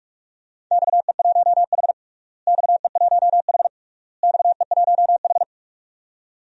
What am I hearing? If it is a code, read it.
Morse code, heard as a keyed tone, about 47 wpm, 700 Hz. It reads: XE1H XE1H XE1H
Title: DXCC-100 Call Signs - 4 Characters - Repeated 3x - 45wpm